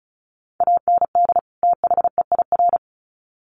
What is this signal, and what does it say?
Morse code, heard as a keyed tone, about 34 wpm, 700 Hz. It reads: AND THEIR
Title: Sets of 2 Words 35wpm